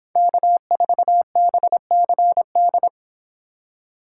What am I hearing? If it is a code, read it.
Morse code, heard as a keyed tone, about 26 wpm, 700 Hz. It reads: K4BCD